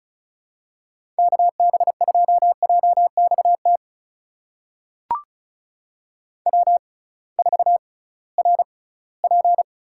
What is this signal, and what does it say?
Morse code, heard as a keyed tone, about 36 wpm, spaced as for 21 wpm, 700 Hz. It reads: KB2JXT W4RP
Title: USA Call Signs - 6x Character Spacing - 35wpm